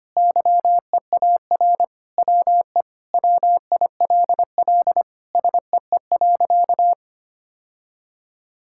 Morse code, heard as a keyed tone, about 25 wpm, 700 Hz. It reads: YEAR WE WILL SEE.
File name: practice / 25wpm / Sentences with 3-Word Phrases - 25wpm